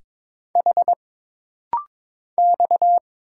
Morse code, heard as a keyed tone, about 22 wpm, 700 Hz. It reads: H X
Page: This rapid-fire timing is geared to push your ICR (Instant Character Recognition) skills.